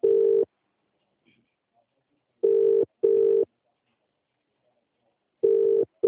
Ringback tone (UK), 400 + 450 Hz.